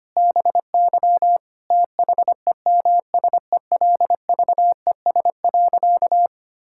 Morse code, heard as a keyed tone, about 25 wpm, 700 Hz. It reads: BY THEMSELVES.
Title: Sentences from Top 1000 Words - Review Part 5 - 25wpm